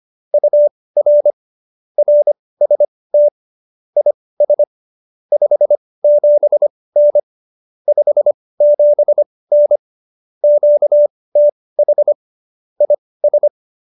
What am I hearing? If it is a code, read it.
Morse code, heard as a keyed tone, about 25 wpm, spaced as for 18 wpm, 590 Hz. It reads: UR RST IS 57N 57N QTH IS
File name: qso201b 18